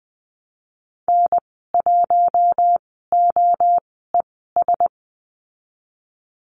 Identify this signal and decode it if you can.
Morse code, heard as a keyed tone, about 20 wpm, spaced as for 15 wpm, 700 Hz. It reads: N1OES
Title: 048 Lesson 9g - Number 4 - Call Signs 20wpm